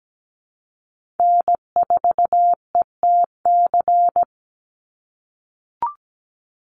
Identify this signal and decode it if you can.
Morse code, heard as a keyed tone, about 17 wpm, 700 Hz. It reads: N4ETC
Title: Call Signs - 17wpm